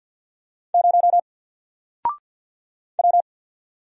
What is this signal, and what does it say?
Morse code, heard as a keyed tone, about 50 wpm, 700 Hz. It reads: 0 W
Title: Single Letter-Number 50wpm